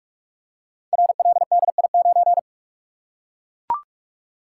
Morse code, heard as a keyed tone, about 44 wpm, 700 Hz. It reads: RLDI9
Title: Code Groups - 5 Letter:Numbers - 45wpm